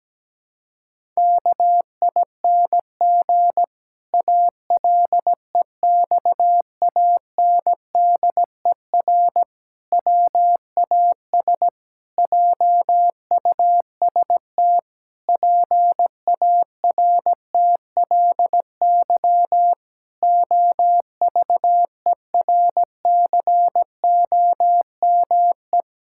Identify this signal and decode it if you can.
Morse code, heard as a keyed tone, about 17 wpm, 700 Hz. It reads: KING ALEXANDER WAS JUST PARTLY OVERCOME
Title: Pangram Challenge v2 - 17wpm